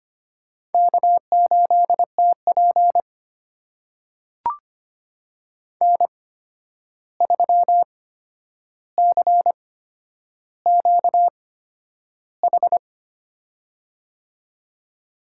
Morse code, heard as a keyed tone, about 25 wpm, spaced as for 10 wpm, 700 Hz. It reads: K8TP N3CQH
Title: USA Call Signs - 8x Character Spacing - 25wpm